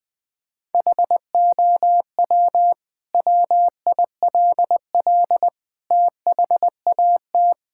Morse code, heard as a keyed tone, about 20 wpm, 700 Hz. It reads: HOW WILL THAT
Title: Sentences from Top 1000 Words - Review Part 6 - 20wpm